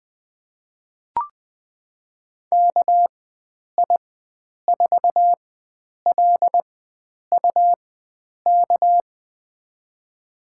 Morse code, heard as a keyed tone, about 20 wpm, spaced as for 9 wpm, 700 Hz. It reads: KI4LUK
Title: USA Call Signs - 6 Characters - 4x Character Spacing - 20wpm